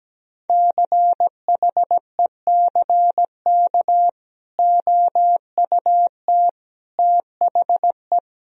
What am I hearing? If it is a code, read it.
Morse code, heard as a keyed tone, about 17 wpm, 700 Hz. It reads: CHECK OUT THE
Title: Sentences from Top 1000 Words - Review Part 4 - 17wpm